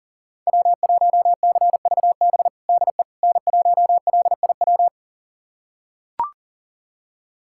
Morse code, heard as a keyed tone, about 37 wpm, 700 Hz. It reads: W1CVB DE N1LIW
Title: USA Callsign DE Callsign - 5 Characters - 40wpm